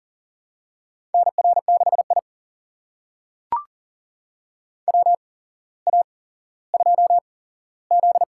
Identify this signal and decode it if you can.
Morse code, heard as a keyed tone, about 40 wpm, spaced as for 17 wpm, 700 Hz. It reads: NR6I WA2Z
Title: USA Call Signs - 4 Characters - 8x Character Spacing - 40wpm